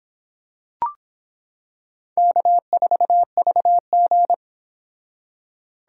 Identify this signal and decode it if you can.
Morse code, heard as a keyed tone, about 25 wpm, 700 Hz. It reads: K4VG